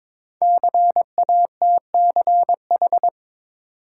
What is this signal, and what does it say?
Morse code, heard as a keyed tone, about 21 wpm, 700 Hz. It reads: CATCH
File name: 5 Letter Words - First 3 Letters Given - 22wpm